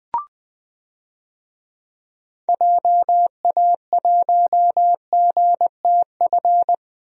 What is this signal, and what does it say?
Morse code, heard as a keyed tone, about 20 wpm, 700 Hz. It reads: JA1GTF
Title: DXCC-100 Call Signs - 6 Characters - 20wpm